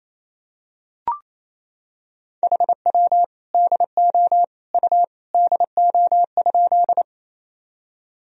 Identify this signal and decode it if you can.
Morse code, heard as a keyed tone, about 28 wpm, 700 Hz. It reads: HW DO U DO?
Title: Sentences from Top 100 Words and Abbreviations 28wpm